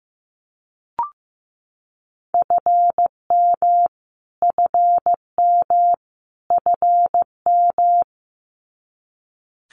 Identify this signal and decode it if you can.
Morse code, heard as a keyed tone, about 14 wpm, 700 Hz. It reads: FM FM FM
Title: Abbreviations Repeated 3x - 15wpm